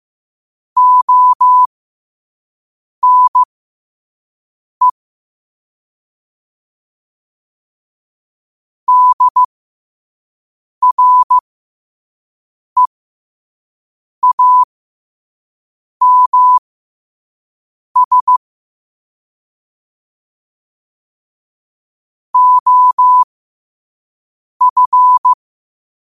Morse code, heard as a keyed tone, about 15 wpm, spaced as for 5 wpm, 1000 Hz. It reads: ONE DREAMS OF
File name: qotdfiles / 5wpm